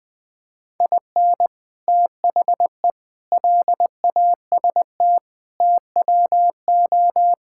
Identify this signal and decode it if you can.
Morse code, heard as a keyed tone, about 20 wpm, 700 Hz. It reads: IN THE LAST TWO
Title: Build A Phrase - 5-Word Phrases - 20wpm